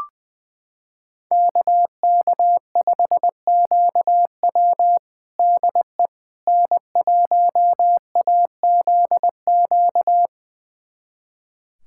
Morse code, heard as a keyed tone, about 21 wpm, 700 Hz. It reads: KK5QW DE N1AZQ
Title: USA Callsign DE Callsign - 5 Characters - 20wpm